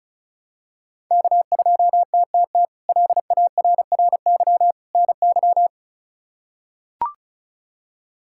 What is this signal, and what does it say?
Morse code, heard as a keyed tone, about 36 wpm, 700 Hz. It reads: K2TTT LARRY NY
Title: K1USN SST QSO Practice - 35wpm